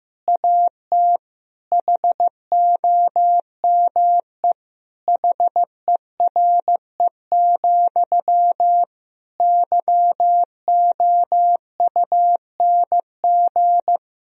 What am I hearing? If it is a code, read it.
Morse code, heard as a keyed tone, about 15 wpm, 700 Hz. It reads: AT HOME HERE, YOUNG
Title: Sentences from Top 1000 Words - Review Part 1 - 15wpm